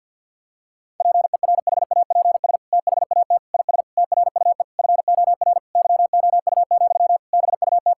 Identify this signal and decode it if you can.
Morse code, heard as a keyed tone, about 50 wpm, 700 Hz. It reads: PERHAPS THAT IS TRUE FOR YOU, BUT
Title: Sentences from Top 1000 Words - Review Part 5 - 50wpm